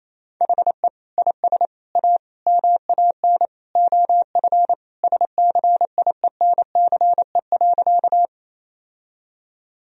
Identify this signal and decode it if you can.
Morse code, heard as a keyed tone, about 28 wpm, 700 Hz. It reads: HE IS A MAN OF SCIENCE.
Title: Sentences from Top 1000 Words - Review Part 2 - 28wpm